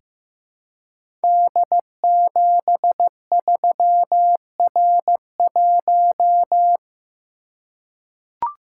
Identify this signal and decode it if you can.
Morse code, heard as a keyed tone, about 15 wpm, 700 Hz. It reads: D73R1
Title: Code Groups - 5 Letter:Numbers - 15wpm